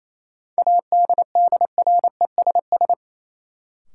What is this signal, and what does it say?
Morse code, heard as a keyed tone, about 28 wpm, 700 Hz. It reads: ADDRESS
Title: Sentences from Top 1000 Words - New Word Drill - 28wpm